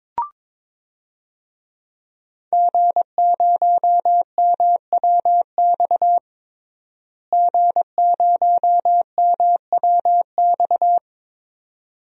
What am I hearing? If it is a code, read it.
Morse code, heard as a keyed tone, about 22 wpm, 700 Hz. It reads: G0MWX G0MWX
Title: DXCC-100 Call Signs - 5 Characters - Repeated 2x - 22wpm